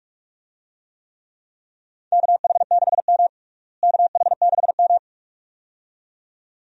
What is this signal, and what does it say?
Morse code, heard as a keyed tone, about 41 wpm, 700 Hz. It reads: KH6M KH6M
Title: DXCC-100 Call Signs - 4 Characters - Repeated 2x - 45wpm